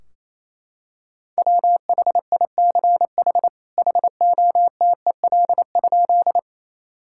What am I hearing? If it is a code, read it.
Morse code, heard as a keyed tone, about 28 wpm, 700 Hz. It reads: WHICH HOTEL?
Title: Sentences from Top 1600 Words 28wpm